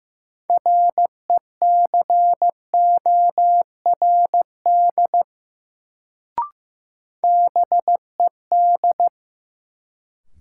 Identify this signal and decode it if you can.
Morse code, heard as a keyed tone, about 15 wpm, 700 Hz. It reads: RECORD BED
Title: Sentences from Top 500 Words - New Word Drill - 15wpm